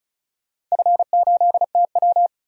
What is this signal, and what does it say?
Morse code, heard as a keyed tone, about 36 wpm, 700 Hz. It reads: F8TW
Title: DXCC-100 Call Signs - 4 Characters - 35wpm